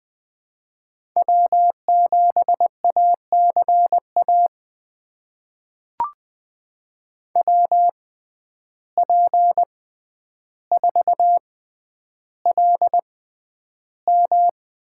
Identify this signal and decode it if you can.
Morse code, heard as a keyed tone, about 20 wpm, spaced as for 10 wpm, 700 Hz. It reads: W7ACA WP4LM
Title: USA Call Signs - 5 Characters - 6x Character Spacing - 20wpm